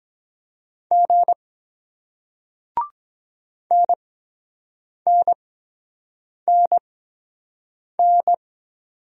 Morse code, heard as a keyed tone, about 23 wpm, spaced as for 7 wpm, 700 Hz. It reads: G NNNN